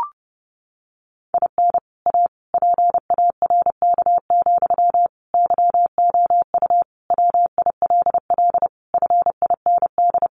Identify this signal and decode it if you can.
Morse code, heard as a keyed tone, about 30 wpm, 700 Hz. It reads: IN A PARK, YOU WILL FIND
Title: Sentences from Top 1300 Words 30wpm